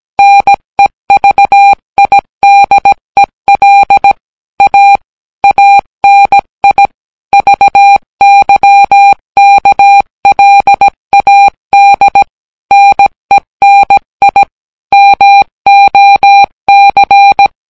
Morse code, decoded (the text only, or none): NEVIDEL A ANI VYKLAD NENI MOC